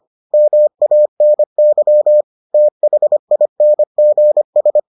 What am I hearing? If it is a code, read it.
Morse code, heard as a keyed tone, about 25 wpm, 600 Hz. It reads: MANY THINGS